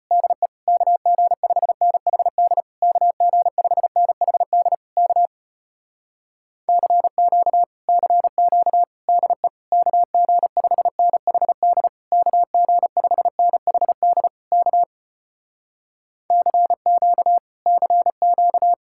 Morse code, heard as a keyed tone, about 33 wpm, 700 Hz. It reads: DE KG5NHD KG5NHD K CQ CQ DE KG5NHD KG5NHD K CQ CQ